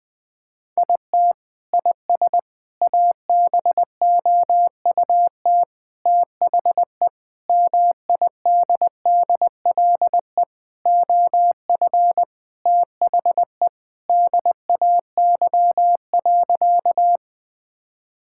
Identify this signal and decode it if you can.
Morse code, heard as a keyed tone, about 20 wpm, 700 Hz. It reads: IT IS ABOUT THE MIDDLE OF THE DAY.
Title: Sentences from Top 1000 Words - Review Part 6 - 20wpm